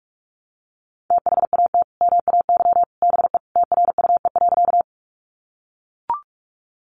Morse code, heard as a keyed tone, about 44 wpm, 700 Hz. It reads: THAT MAY BE TRUE.
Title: Sentences from Top 400 Words 45wpm